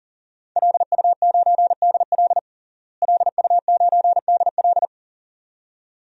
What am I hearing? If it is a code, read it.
Morse code, heard as a keyed tone, about 37 wpm, 700 Hz. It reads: LU9DL LU9DL
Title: DXCC-100 Call Signs - 5 Characters - Repeated 2x - 40wpm